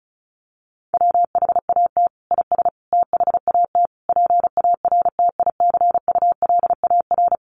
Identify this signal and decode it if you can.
Morse code, heard as a keyed tone, about 34 wpm, 700 Hz. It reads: WHAT IS THAT PARTICULAR
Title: Sentences from Top 1500 Words 35wpm